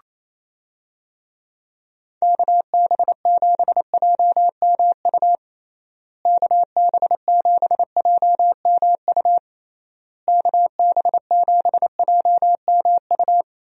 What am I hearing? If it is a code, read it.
Morse code, heard as a keyed tone, about 27 wpm, 700 Hz. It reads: KB7JMU KB7JMU KB7JMU
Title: USA Call Signs - 6 Characters - Repeated 3x - 28wpm